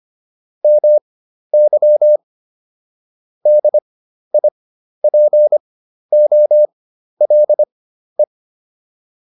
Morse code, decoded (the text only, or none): MY DIPOLE